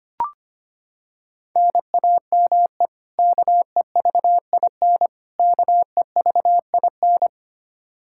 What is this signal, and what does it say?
Morse code, heard as a keyed tone, about 25 wpm, 700 Hz. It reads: NAME KEVIN KEVIN
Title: Practice QSO - Exchange 1 - Beginning - 25wpm